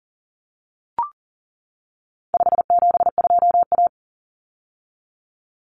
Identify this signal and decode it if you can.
Morse code, heard as a keyed tone, about 40 wpm, 700 Hz. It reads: 572A
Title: Common Cut Numbers - 4-Digit 40wpm